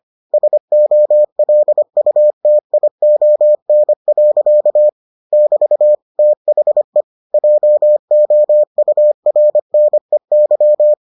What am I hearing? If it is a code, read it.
Morse code, heard as a keyed tone, about 25 wpm, 600 Hz. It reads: SOLUTION. <BT> THE JOURNEY